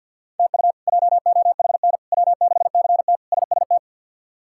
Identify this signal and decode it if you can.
Morse code, heard as a keyed tone, about 48 wpm, 700 Hz. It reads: TU JOHN W6CT SST
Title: K1USN SST QSO Practice - 50wpm